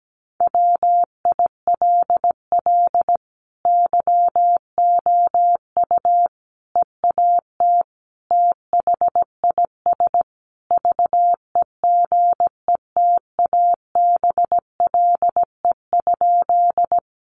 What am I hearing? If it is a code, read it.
Morse code, heard as a keyed tone, about 17 wpm, 700 Hz. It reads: WILL YOU EAT THIS VEGETABLE?
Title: Sentences from Top 1300 Words 17wpm